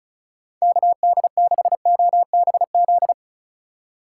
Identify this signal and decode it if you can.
Morse code, heard as a keyed tone, about 36 wpm, 700 Hz. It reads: KD6OBZ
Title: USA Call Signs - 3x Character Spacing - 35wpm